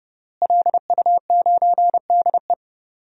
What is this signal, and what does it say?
Morse code, heard as a keyed tone, about 32 wpm, 700 Hz. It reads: LU9DE
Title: DXCC-100 Call Signs - 5 Characters - Repeated 3x - 30wpm